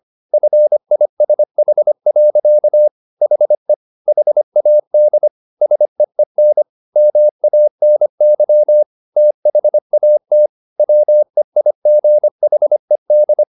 Morse code, heard as a keyed tone, about 25 wpm, 600 Hz. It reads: FISH. HE HAD SEEN MANY THAT WEIGHED
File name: the old man and the sea part 1 25wpm